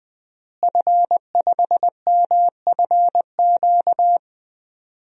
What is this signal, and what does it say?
Morse code, heard as a keyed tone, about 21 wpm, 700 Hz. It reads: F5MFQ
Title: DXCC-100 Call Signs - 5 Characters - Repeated 2x - 20wpm